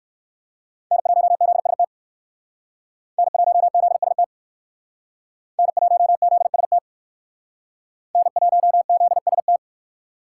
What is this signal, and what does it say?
Morse code, heard as a keyed tone, about 57 wpm, 700 Hz. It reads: N1ZST N1ZST N1ZST N1ZST